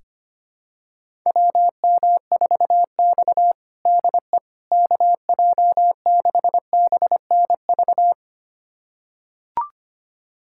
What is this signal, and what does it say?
Morse code, heard as a keyed tone, about 25 wpm, 700 Hz. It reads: WM4X DE KJ6BNV
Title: USA Callsign DE Callsign 25wpm